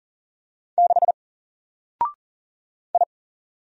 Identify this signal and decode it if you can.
Morse code, heard as a keyed tone, about 40 wpm, 700 Hz. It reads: 6 I
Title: Single Character (Letter:Number:Punctuation:Prosign) Alt 40wpm